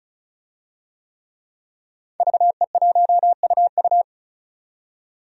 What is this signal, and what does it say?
Morse code, heard as a keyed tone, about 34 wpm, 700 Hz. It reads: VE1UU
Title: DXCC-100 Call Signs - 5 Characters - 35wpm